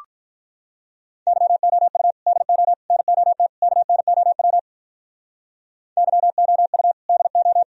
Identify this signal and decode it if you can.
Morse code, heard as a keyed tone, about 43 wpm, 700 Hz. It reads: YOU DO NOT KNOW YOU DO
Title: Speed-Racing - Sets of 4 Words 35wpm